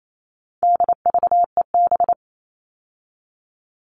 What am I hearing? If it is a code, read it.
Morse code, heard as a keyed tone, about 28 wpm, 700 Hz. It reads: DVEB
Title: All Cut Numbers - 4-Digit 28wpm